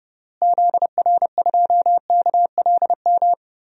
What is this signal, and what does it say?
Morse code, heard as a keyed tone, about 30 wpm, 700 Hz. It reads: ZR2KLM
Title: DXCC-100 Call Signs - 6 Characters - Repeated 2x - 30wpm